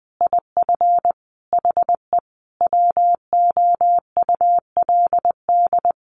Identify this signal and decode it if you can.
Morse code, heard as a keyed tone, about 20 wpm, 700 Hz. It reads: IF HE WOULD
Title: Sentences from Top 1300 Words 20wpm